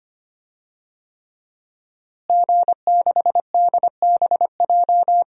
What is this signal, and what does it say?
Morse code, heard as a keyed tone, about 25 wpm, 700 Hz. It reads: G6DBJ
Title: DXCC-100 Call Signs - 5 Characters - Repeated 3x - 25wpm